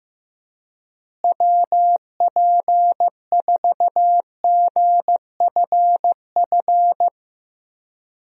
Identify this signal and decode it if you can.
Morse code, heard as a keyed tone, about 15 wpm, 700 Hz. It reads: WP4GFF
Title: USA Call Signs - 6 Characters - Repeated 3x - 15wpm